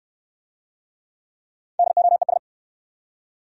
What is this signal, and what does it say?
Morse code, heard as a keyed tone, about 65 wpm, 700 Hz. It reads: DOES